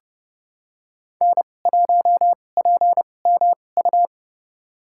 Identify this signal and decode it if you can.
Morse code, heard as a keyed tone, about 32 wpm, spaced as for 22 wpm, 700 Hz. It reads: N1PMU
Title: USA Call Signs - 2x Character Spacing - 30wpm